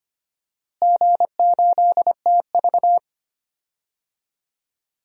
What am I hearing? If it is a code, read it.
Morse code, heard as a keyed tone, about 25 wpm, 700 Hz. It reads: G8TV
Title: DXCC-100 Call Signs - 4 Characters - Repeated 3x - 25wpm